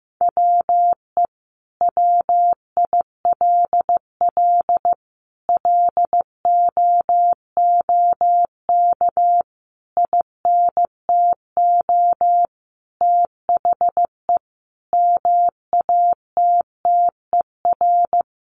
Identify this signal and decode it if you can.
Morse code, heard as a keyed tone, about 14 wpm, 700 Hz. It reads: WE WILL LOOK INTO THE MATTER
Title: Sentences from Top 1400 Words 15wpm